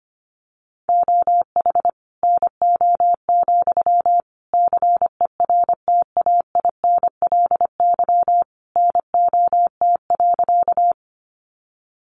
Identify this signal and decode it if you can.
Morse code, heard as a keyed tone, about 25 wpm, 700 Hz. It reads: OH NO, CERTAINLY NOT.
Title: Sentences from Top 900 Words 25wpm